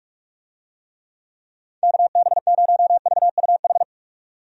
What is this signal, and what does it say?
Morse code, heard as a keyed tone, about 41 wpm, 700 Hz. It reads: KB0VUH
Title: USA Call Signs - 6 Characters - 45wpm